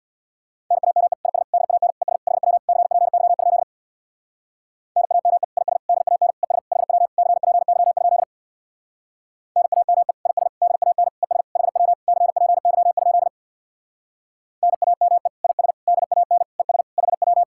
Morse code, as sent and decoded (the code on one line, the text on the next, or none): -. .- -- . / .. ... / -.. .- -. / . ... / .... .-- / -.-. .--. -.-- ..--.. / -. .- -- . / .. ... / -.. .- -. / . ... / .... .-- / -.-. .--. -.-- ..--.. / -. .- -- . / .. ... / -.. .- -. / . ... / .... .-- / -.-. .--. -.-- ..--.. / -. .- -- . / .. ... / -.. .- -. / . ... / .... .--
NAME IS DAN ES HW CPY? NAME IS DAN ES HW CPY? NAME IS DAN ES HW CPY? NAME IS DAN ES HW